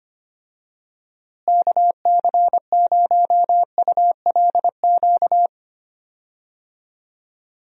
Morse code, heard as a keyed tone, about 25 wpm, 700 Hz. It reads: KC0ULQ
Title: USA Call Signs - 6 Characters - Repeated 3x - 25wpm